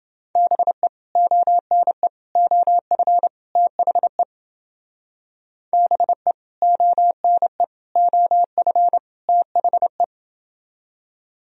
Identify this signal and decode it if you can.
Morse code, heard as a keyed tone, about 29 wpm, 700 Hz. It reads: BE ONE OF THE BE ONE OF THE